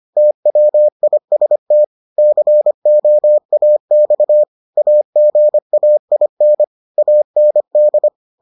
Morse code, heard as a keyed tone, about 25 wpm, 600 Hz. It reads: TWIST COAX AGAIN AND